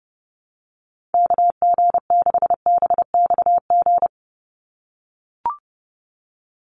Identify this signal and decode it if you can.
Morse code, heard as a keyed tone, about 30 wpm, 700 Hz. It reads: KG6BXG
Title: USA Call Signs 30wpm